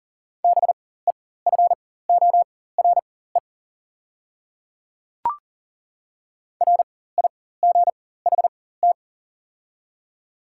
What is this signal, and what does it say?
Morse code, heard as a keyed tone, about 40 wpm, spaced as for 19 wpm, 690 Hz. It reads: BEFORE RIGHT
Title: 4x Character Spacing - Top 200 Words - 40wpm